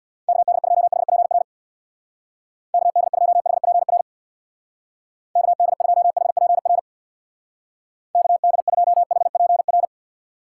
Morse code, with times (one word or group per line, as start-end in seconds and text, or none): 0.3-1.4 KD2HPR
2.7-4.0 KD2HPR
5.4-6.8 KD2HPR
8.2-9.8 KD2HPR